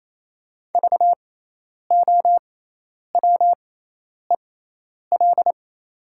Morse code, heard as a keyed tone, about 27 wpm, spaced as for 10 wpm, 690 Hz. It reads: VOWEL
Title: 6x Character Spacing - Top 500 Words - 28wpm